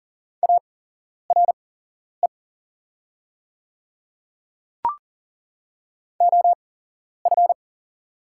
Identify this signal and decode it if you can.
Morse code, heard as a keyed tone, about 40 wpm, spaced as for 11 wpm, 700 Hz. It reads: ARE OF
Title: 8x Character Spacing - Top 100 Words - 40wpm